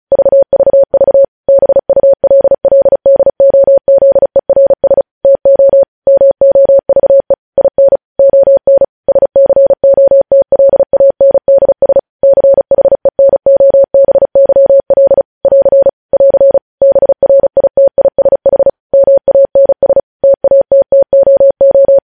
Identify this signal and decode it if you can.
Morse code, heard as a keyed tone, about 36 wpm, 550 Hz. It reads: VVV BULLDOZERS TO MOVE IN ON SCOTLANDS CHENOBYL <AR> <AR> BRITISH MANS TATTOO
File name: podcast / News headlines-1x-35x35